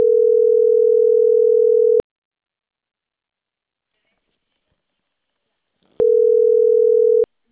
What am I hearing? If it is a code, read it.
Ringback tone (North America), 440 + 480 Hz.